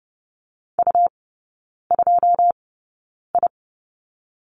Morse code, heard as a keyed tone, about 30 wpm, spaced as for 9 wpm, 700 Hz. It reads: U2I
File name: Lesson 17e - Letter u - Three Characters 30wpm